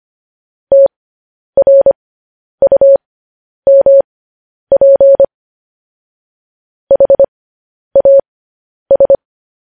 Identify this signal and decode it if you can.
Morse code, heard as a keyed tone, about 25 wpm, spaced as for 10 wpm, 550 Hz. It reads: TRUMP HAS